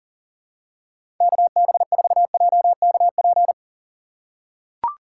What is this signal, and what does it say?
Morse code, heard as a keyed tone, about 40 wpm, 700 Hz. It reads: KB4JKP
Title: USA Call Signs - 6 Characters - 6x Character Spacing - 40wpm